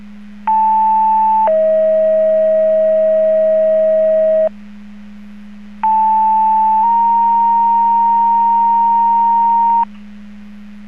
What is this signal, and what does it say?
Two-tone page, heard as a one-second tone, then a three-second tone, about 880 then 640 Hz, about 880 then 930 Hz. It is heard from 0.5 to 9.8 s.